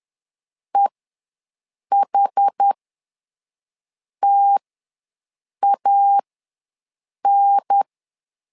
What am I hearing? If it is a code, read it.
Morse code, heard as a keyed tone, about 10 wpm, spaced as for 5 wpm, 800 Hz. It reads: EHTAN
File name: LEZIONI / 3 MWRO / 3 MW ws 10wpm 2213